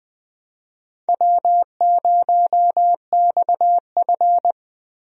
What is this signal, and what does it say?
Morse code, heard as a keyed tone, about 21 wpm, 700 Hz. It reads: W0XF
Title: USA Call Signs - 4 Characters - Repeated 3x - 20wpm